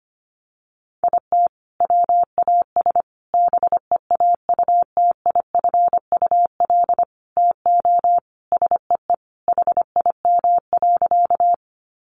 Morse code, heard as a keyed tone, about 25 wpm, 700 Hz. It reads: IT WAS BEAUTIFUL TO SEE HIM.
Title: Sentences from Top 900 Words 25wpm